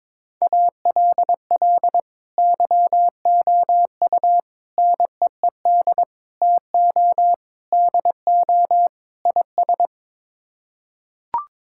Morse code, heard as a keyed tone, about 22 wpm, 700 Hz. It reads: ALL YOU NEED TO DO IS
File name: practice / 22wpm / Sets of 6 Words - 22wpm